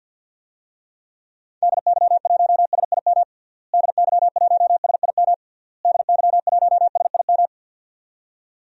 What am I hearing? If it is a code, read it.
Morse code, heard as a keyed tone, about 52 wpm, 700 Hz. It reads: DY1SIM DY1SIM DY1SIM
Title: DXCC-100 Call Signs - 6 Characters - Repeated 3x - 50wpm